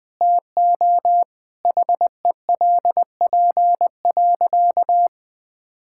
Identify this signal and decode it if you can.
Morse code, heard as a keyed tone, about 20 wpm, 700 Hz. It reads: TO HELP.
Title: Sentences with 9-Word Phrases - 20wpm